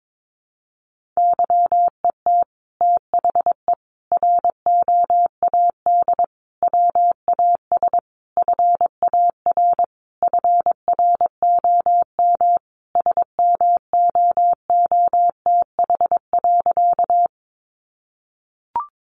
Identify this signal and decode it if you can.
Morse code, heard as a keyed tone, about 22 wpm, 700 Hz. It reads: YET THE ROAD WAS FAR FROM SMOOTH.
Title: Sentences from Top 1400 Words 22wpm